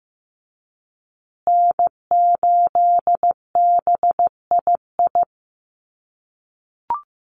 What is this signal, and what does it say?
Morse code, heard as a keyed tone, about 15 wpm, 700 Hz. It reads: N8BII
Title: USA Call Signs Repeated 2x 15wpm